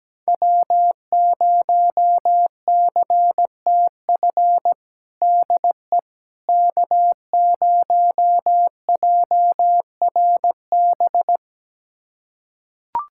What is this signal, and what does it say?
Morse code, heard as a keyed tone, about 17 wpm, 700 Hz. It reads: W0CTF DE K0JRB
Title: USA Callsign DE Callsign - 5 Characters - 17wpm